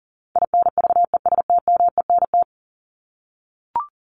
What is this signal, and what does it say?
Morse code, heard as a keyed tone, about 40 wpm, 700 Hz. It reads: INVESTMENT
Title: Top 501-750 Words 40wpm